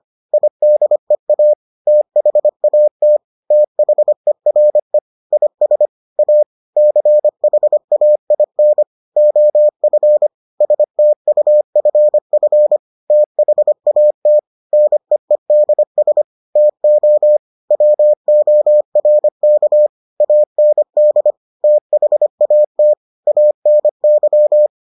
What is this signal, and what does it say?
Morse code, heard as a keyed tone, about 25 wpm, 600 Hz. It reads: IDEA THAT THERE IS A CHAIN OF STUFF THAT NEEDS TO WORK AND THAT ANY